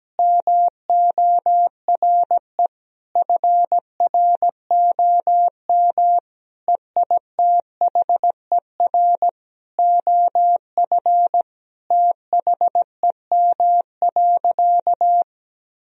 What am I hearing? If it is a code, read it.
Morse code, heard as a keyed tone, about 17 wpm, 700 Hz. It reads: MORE FROM EITHER OF THEM.
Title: Sentences from Top 1000 Words - Review Part 4 - 17wpm